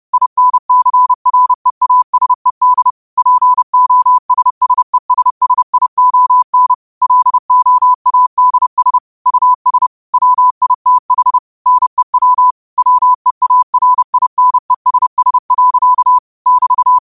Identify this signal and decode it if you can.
Morse code, heard as a keyed tone, about 30 wpm, 1000 Hz. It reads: INCREASED POSSESSION LOADS US WITH NEW WEARINESS. <BT>
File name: qotdfiles / 30wpm